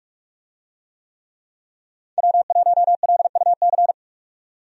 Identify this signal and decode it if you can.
Morse code, heard as a keyed tone, about 46 wpm, 700 Hz. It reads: W1LUC
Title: USA Call Signs - 5 Characters - 45wpm